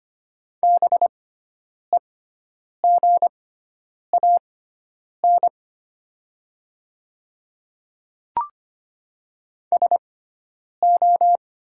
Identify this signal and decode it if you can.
Morse code, heard as a keyed tone, about 25 wpm, spaced as for 9 wpm, 700 Hz. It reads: BEGAN SO
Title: 6x Character Spacing - Top 300 Words - 25wpm